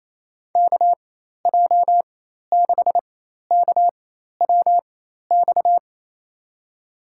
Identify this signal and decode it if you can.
Morse code, heard as a keyed tone, about 27 wpm, spaced as for 13 wpm, 700 Hz. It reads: KJ6KWX
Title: USA Call Signs - 6 Characters - 4x Character Spacing - 28wpm